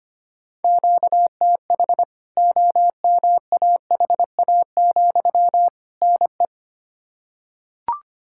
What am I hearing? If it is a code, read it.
Morse code, heard as a keyed tone, about 25 wpm, 700 Hz. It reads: QTH OMAHA, NE
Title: K1USN SST QSO Practice 25wpm